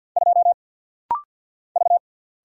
Morse code, heard as a keyed tone, about 50 wpm, 700 Hz. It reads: 2 V
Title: Single Letter-Number Alt - Rapid-Fire 50wpm